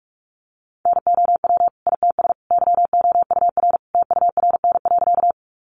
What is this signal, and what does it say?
Morse code, heard as a keyed tone, about 44 wpm, 700 Hz. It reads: NOW ITS YOUR TURN.
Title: Sentences from Top 600 Words 45wpm